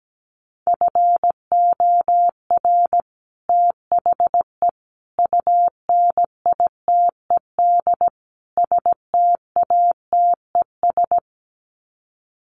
Morse code, heard as a keyed tone, about 17 wpm, 700 Hz. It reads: FOR THE UNITED STATES
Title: Sets of 4 Words 17wpm